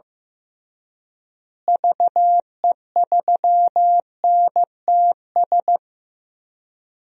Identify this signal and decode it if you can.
Morse code, heard as a keyed tone, about 15 wpm, 700 Hz. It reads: VE3NTS
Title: DXCC-100 Call Signs - 6 Characters - Repeated 3x - 15wpm